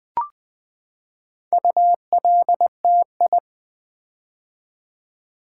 Morse code, heard as a keyed tone, about 20 wpm, 700 Hz. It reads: ULTI
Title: Build A Word - 4 Letters At A Time - 10-Letter Words - 20wpm